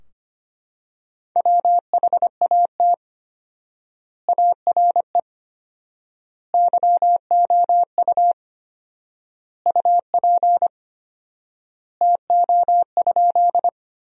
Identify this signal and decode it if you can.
Morse code, heard as a keyed tone, about 25 wpm, 700 Hz. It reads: WHAT ARE YOU UP TO?
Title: 4x Word-Spacing - Sentences from Top 100 Words - 25wpm